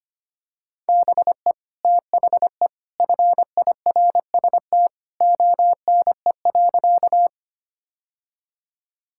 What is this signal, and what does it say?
Morse code, heard as a keyed tone, about 25 wpm, 700 Hz. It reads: BE THE FIRST ONE.
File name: practice / 25wpm / Sentences with 2-Word Phrases - 25wpm